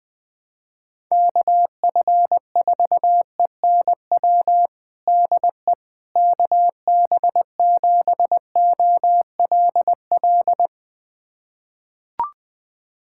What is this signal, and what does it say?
Morse code, heard as a keyed tone, about 21 wpm, 700 Hz. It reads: KF4ENW DE KB7OLL
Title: USA Callsign DE Callsign - 6 Characters - 20wpm